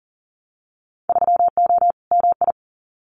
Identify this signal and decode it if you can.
Morse code, heard as a keyed tone, about 40 wpm, 700 Hz. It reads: 3O MI
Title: Winter Field Day Exchanges 40wpm